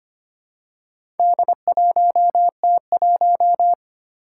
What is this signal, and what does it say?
Morse code, heard as a keyed tone, about 25 wpm, 700 Hz. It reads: D1T1
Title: Four Challenging Letter-Number Combinations 25wpm